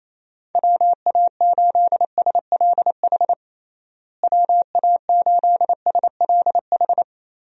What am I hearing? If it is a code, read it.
Morse code, heard as a keyed tone, about 28 wpm, 700 Hz. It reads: WA8SLH WA8SLH
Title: USA Call Signs - 6 Characters - Repeated 3x - 28wpm